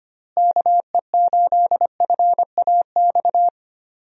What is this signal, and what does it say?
Morse code, heard as a keyed tone, about 25 wpm, 700 Hz. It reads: KE8FAX
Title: USA Call Signs - 6 Characters - Repeated 3x - 25wpm